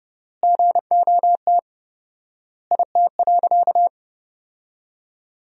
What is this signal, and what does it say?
Morse code, heard as a keyed tone, about 30 wpm, 700 Hz. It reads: GOT IT.
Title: 4x Word-Spacing - Sentences from Top 300 Words - 30wpm